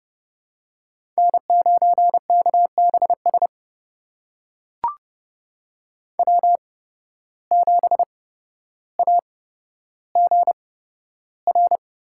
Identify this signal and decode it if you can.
Morse code, heard as a keyed tone, about 32 wpm, spaced as for 13 wpm, 700 Hz. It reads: N9KBS W7AGR
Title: USA Call Signs - 5 Characters - 8x Character Spacing - 30wpm